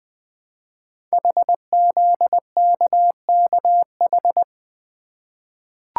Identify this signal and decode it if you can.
Morse code, heard as a keyed tone, about 20 wpm, 700 Hz. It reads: HZKKH
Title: Code Groups - 5 Letters - 20wpm